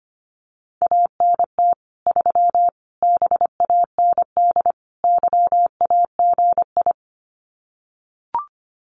Morse code, heard as a keyed tone, about 25 wpm, 700 Hz. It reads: ANT 3 BAND YAGI
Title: Common QSO Phrases 25wpm